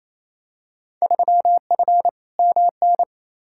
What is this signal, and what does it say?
Morse code, heard as a keyed tone, about 28 wpm, 700 Hz. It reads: 3F MN
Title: Field Day Exchanges Encore - 28wpm